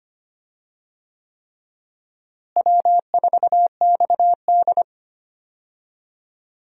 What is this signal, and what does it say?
Morse code, heard as a keyed tone, about 25 wpm, 700 Hz. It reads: W4XD
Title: USA Call Signs - 4 Characters - Repeated 3x - 25wpm